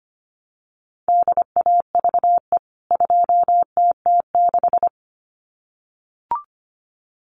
Morse code, heard as a keyed tone, about 25 wpm, 700 Hz. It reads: DAVE 2TT6
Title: CWT Exchange 25wpm